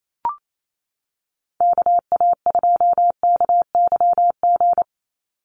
Morse code, heard as a keyed tone, about 27 wpm, 700 Hz. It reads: KA2KYG
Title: USA Call Signs 28wpm